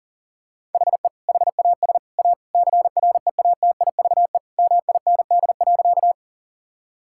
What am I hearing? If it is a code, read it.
Morse code, heard as a keyed tone, about 37 wpm, 700 Hz. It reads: HE HAS A CREATIVE MIND.
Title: Sentences from Top 1000 Words - Review Part 5 - 40wpm